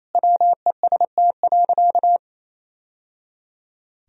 Morse code, heard as a keyed tone, about 28 wpm, 700 Hz. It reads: WEST.
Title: Sentences from Top 1000 Words - Review Part 3 - 28wpm